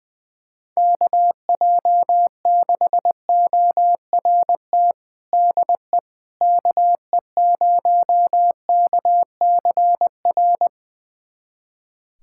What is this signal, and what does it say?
Morse code, heard as a keyed tone, about 21 wpm, 700 Hz. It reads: KJ6ORT DE KE0KCR
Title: USA Callsign DE Callsign - 6 Characters - 20wpm